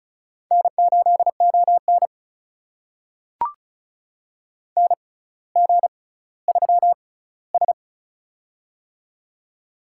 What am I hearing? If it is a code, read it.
Morse code, heard as a keyed tone, about 36 wpm, spaced as for 18 wpm, 700 Hz. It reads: N8ON NG3S
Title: USA Call Signs - 6x Character Spacing - 35wpm